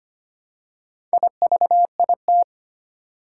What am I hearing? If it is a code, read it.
Morse code, heard as a keyed tone, about 25 wpm, 700 Hz. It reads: IVIT
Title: Four Challenging Letter-Number Combinations 25wpm